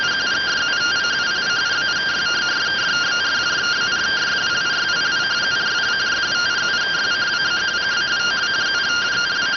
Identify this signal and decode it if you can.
Two-tone teleprinter signal, about 170 Hz shift.